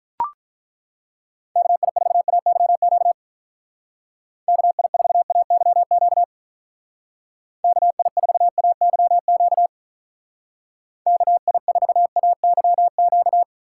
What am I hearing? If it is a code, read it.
Morse code, heard as a keyed tone, about 42 wpm, 700 Hz. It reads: KI4AYQ KI4AYQ KI4AYQ KI4AYQ
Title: Speed Racing - USA Call Signs 35wpm